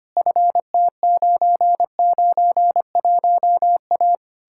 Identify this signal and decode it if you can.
Morse code, heard as a keyed tone, about 33 wpm, 700 Hz. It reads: FT991A